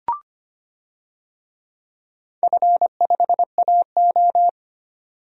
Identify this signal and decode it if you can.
Morse code, heard as a keyed tone, about 25 wpm, 700 Hz. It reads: F5AO
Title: DXCC-100 Call Signs - 4 Characters - 25wpm